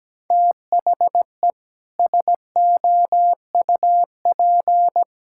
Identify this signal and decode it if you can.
Morse code, heard as a keyed tone, about 17 wpm, 700 Hz. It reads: THE SOUP
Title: Sentences from Top 1000 Words - Review Part 6 - 17wpm